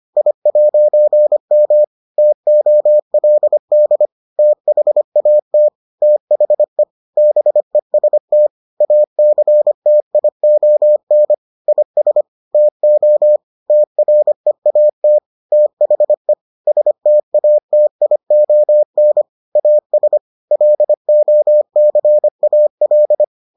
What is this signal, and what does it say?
Morse code, heard as a keyed tone, about 25 wpm, 600 Hz. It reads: I'M TOLD THAT THE BEST ACTION IS TO TREAT THE STATION AS LOCAL